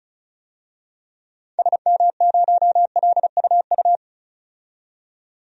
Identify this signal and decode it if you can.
Morse code, heard as a keyed tone, about 36 wpm, 700 Hz. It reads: SM0LUU
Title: DXCC-100 Call Signs - 6 Characters - 35wpm